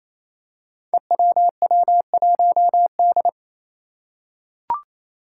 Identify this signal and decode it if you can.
Morse code, heard as a keyed tone, about 27 wpm, 700 Hz. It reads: EWW1D
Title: Five Challenging Letter-Number Combinations 28wpm